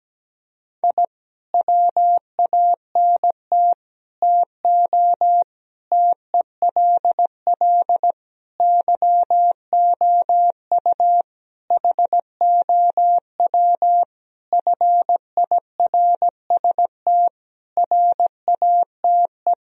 Morse code, heard as a keyed tone, about 17 wpm, 700 Hz. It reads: I WANT TO TELL YOU HOW FIRST RATE
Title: Sentences from Top 1000 Words - Review Part 4 - 17wpm